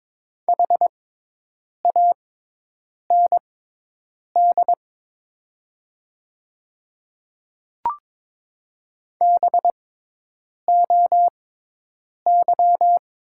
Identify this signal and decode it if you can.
Morse code, heard as a keyed tone, about 22 wpm, spaced as for 8 wpm, 700 Hz. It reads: HAND BOY
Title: 6x Character Spacing - Top 200 Words - 22wpm